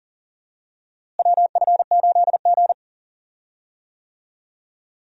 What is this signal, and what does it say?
Morse code, heard as a keyed tone, about 40 wpm, 700 Hz. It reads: WF8G
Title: USA Call Signs - 4 Characters - Repeated 3x - 40wpm